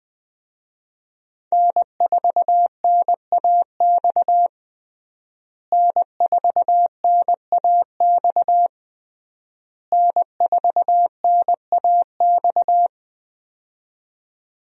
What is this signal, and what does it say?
Morse code, heard as a keyed tone, about 21 wpm, 700 Hz. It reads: N4NAX N4NAX N4NAX
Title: USA Call Signs - 5 Characters - Repeated 3x - 20wpm